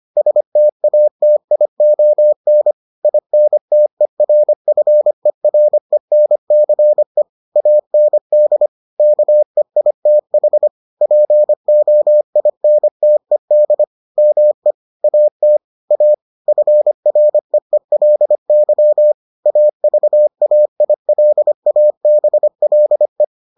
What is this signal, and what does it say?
Morse code, heard as a keyed tone, about 25 wpm, 600 Hz. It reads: STATION INTERFERENCE AND KEITH POINTED ME AT A FREELY AVAILABLE